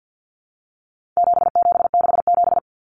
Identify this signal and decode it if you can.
Morse code, heard as a keyed tone, about 50 wpm, 700 Hz. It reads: 7767
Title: Numbers - 4-Digit 50wpm